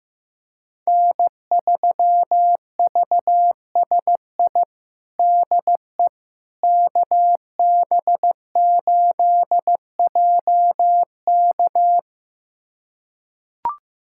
Morse code, heard as a keyed tone, about 15 wpm, 700 Hz. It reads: N3VSI DE KB8JK
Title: USA Callsign DE Callsign - 5 Characters - 15wpm